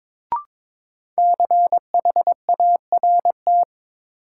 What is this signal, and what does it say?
Morse code, heard as a keyed tone, about 21 wpm, 700 Hz. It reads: CHART
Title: Sentences from Top 1300 Words - New Word Drill - 22wpm